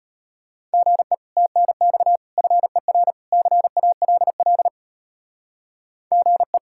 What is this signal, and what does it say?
Morse code, heard as a keyed tone, about 30 wpm, 700 Hz. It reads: GE TNX FER CALL GE
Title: Speed-Racing Common QSO Phrases - 25wpm